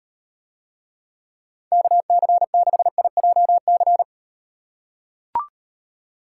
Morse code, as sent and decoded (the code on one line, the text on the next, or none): -.- -.-. -.... .. .--- -.-.
KC6IJC